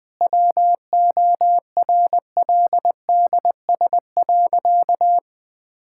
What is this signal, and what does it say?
Morse code, heard as a keyed tone, about 21 wpm, 700 Hz. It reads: WORLDS.
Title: Difficult Sentences with Hints - 20wpm